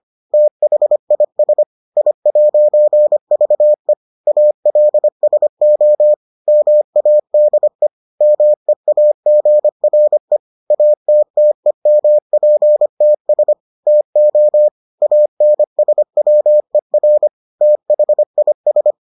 Morse code, heard as a keyed tone, about 25 wpm, 600 Hz. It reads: THIS I'VE ALSO MADE MEAGRE ATTEMPTS TO ANSWER THIS